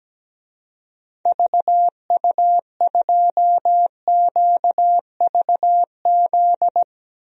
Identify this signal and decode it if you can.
Morse code, heard as a keyed tone, about 17 wpm, 700 Hz. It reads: VU2QVZ
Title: DXCC-100 Call Signs - 6 Characters - Repeated 3x - 17wpm